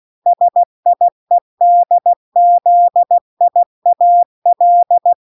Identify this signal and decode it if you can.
Morse code, heard as a keyed tone, about 16 wpm, 700 Hz. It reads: SIEDZIAL